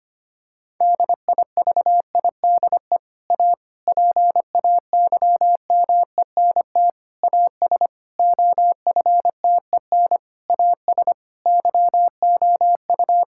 Morse code, heard as a keyed tone, about 25 wpm, 700 Hz. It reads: DIVIDE A PAYMENT AS OFTEN AS YOU
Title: Sentences from Top 1600 Words 25wpm